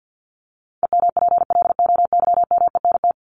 Morse code, heard as a keyed tone, about 50 wpm, 700 Hz. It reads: EMPLOYMENT
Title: Top 1251-1500 Words 50wpm